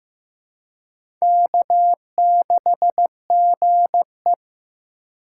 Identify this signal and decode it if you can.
Morse code, heard as a keyed tone, about 15 wpm, 700 Hz. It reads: K6GE
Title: USA Call Signs - 4 Characters - 2x Character Spacing - 15wpm